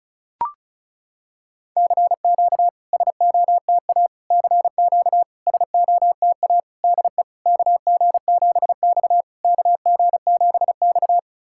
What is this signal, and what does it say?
Morse code, heard as a keyed tone, about 36 wpm, 690 Hz. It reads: CQ SOTA CQ SOTA DE KG7X KG7X
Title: SOTA QSO Practice 35wpm